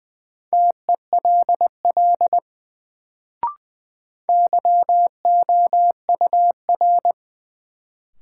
Morse code, heard as a keyed tone, about 20 wpm, 700 Hz. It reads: TELL YOUR
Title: Sets of 2 Words - New Word Drill - 20wpm